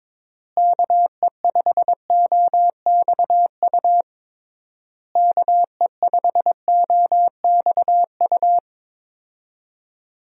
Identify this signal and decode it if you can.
Morse code, heard as a keyed tone, about 22 wpm, 700 Hz. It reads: KE5OXU KE5OXU
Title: USA Call Signs - 6 Characters - Repeated 3x - 22wpm